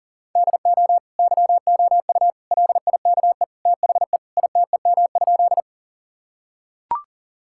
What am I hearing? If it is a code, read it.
Morse code, heard as a keyed tone, about 40 wpm, 700 Hz. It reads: DO YOU LIKE THE ITEM?
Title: Sentences from Top 1000 Words - Review Part 6 - 40wpm